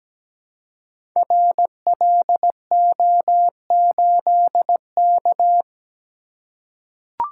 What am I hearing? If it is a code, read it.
Morse code, heard as a keyed tone, about 17 wpm, 700 Hz. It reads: RLO8K
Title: Code Groups - 5 Letter:Numbers - 17wpm